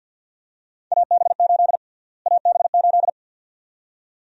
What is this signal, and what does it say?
Morse code, heard as a keyed tone, about 49 wpm, 700 Hz. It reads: AB8 AB8
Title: USA Call Sign Prefixes - Repeated 2x - 50wpm